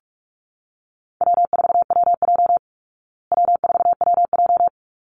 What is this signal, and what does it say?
Morse code, heard as a keyed tone, about 44 wpm, 700 Hz. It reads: W4WJ W4WJ
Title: Call Signs Repeated 2x - 45wpm